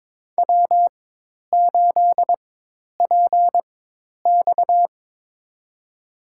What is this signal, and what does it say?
Morse code, heard as a keyed tone, about 21 wpm, spaced as for 10 wpm, 700 Hz. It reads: W8PX
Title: USA Call Signs - 4 Characters - 4x Character Spacing - 22wpm